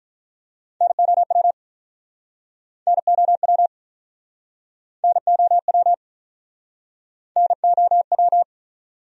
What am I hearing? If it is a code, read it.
Morse code, heard as a keyed tone, about 48 wpm, 700 Hz. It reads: NOW NOW NOW NOW